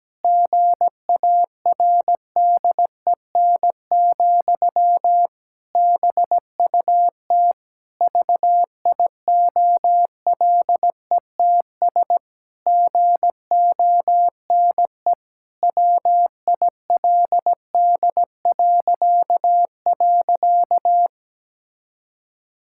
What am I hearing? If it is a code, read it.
Morse code, heard as a keyed tone, about 17 wpm, 700 Hz. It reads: GARDEN, BUT VIOLETS GONE WILD..
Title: Pangram Challenge v2 - 17wpm